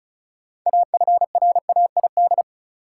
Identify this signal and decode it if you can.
Morse code, heard as a keyed tone, about 34 wpm, 700 Hz. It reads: AFRAID
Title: Sentences from Top 1500 Words - New Word Drill Alt - 35wpm